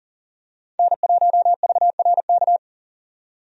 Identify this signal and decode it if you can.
Morse code, heard as a keyed tone, about 40 wpm, 700 Hz. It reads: N1VRK
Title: USA Call Signs - 5 Characters - 40wpm